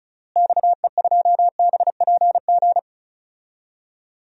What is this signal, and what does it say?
Morse code, heard as a keyed tone, about 36 wpm, 700 Hz. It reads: XE2BPG
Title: DXCC-100 Call Signs - 6 Characters - Repeated 3x - 35wpm